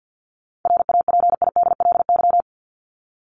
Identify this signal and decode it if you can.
Morse code, heard as a keyed tone, about 50 wpm, 700 Hz. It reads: RAPIDLY
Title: LY Suffix - Top 100 Words with Base Words 50wpm